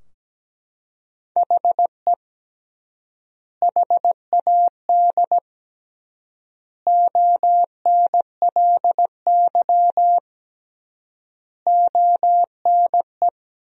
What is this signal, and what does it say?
Morse code, heard as a keyed tone, about 17 wpm, 700 Hz. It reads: HE HAD ONLY ONE
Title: 3x Word-Spacing - Sentences from Top 300 Words - 17wpm